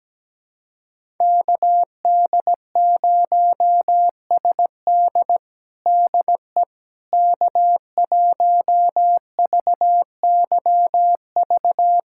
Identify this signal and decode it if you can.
Morse code, heard as a keyed tone, about 17 wpm, 700 Hz. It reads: KD0SD DE K1VYV
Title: USA Callsign DE Callsign 17wpm